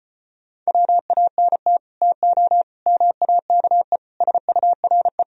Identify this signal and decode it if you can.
Morse code, heard as a keyed tone, about 33 wpm, 700 Hz. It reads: WANT TO MAKE SURE